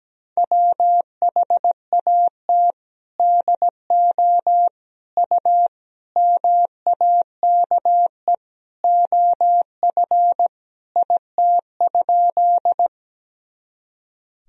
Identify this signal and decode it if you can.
Morse code, heard as a keyed tone, about 17 wpm, 700 Hz. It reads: WHAT DO U MAKE OF IT?
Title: Sentences from Top 100 Words and Abbreviations 17wpm